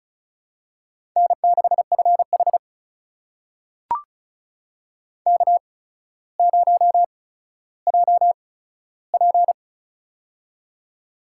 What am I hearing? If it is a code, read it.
Morse code, heard as a keyed tone, about 35 wpm, spaced as for 15 wpm, 700 Hz. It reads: N6FH K0JP
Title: USA Call Signs - 4 Characters - 8x Character Spacing - 35wpm